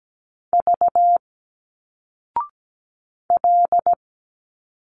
Morse code, heard as a keyed tone, about 17 wpm, 700 Hz. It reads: V L
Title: Single Letters 17wpm